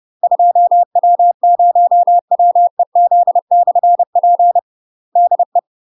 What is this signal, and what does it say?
Morse code, heard as a keyed tone, about 30 wpm, 700 Hz. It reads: 2W0WEZ/P DE
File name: UK MASTER QSOs TRAINING / 30WPM / UK MASTER QSOs 0007